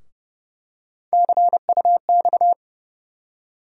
Morse code, heard as a keyed tone, about 31 wpm, 700 Hz. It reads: CUX
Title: USA Call Sign Postfixes - 3 Letters 30wpm